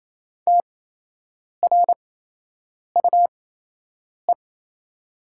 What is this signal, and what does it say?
Morse code, heard as a keyed tone, about 27 wpm, spaced as for 8 wpm, 690 Hz. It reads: TRUE
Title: 8x Character Spacing - Top 500 Words - 28wpm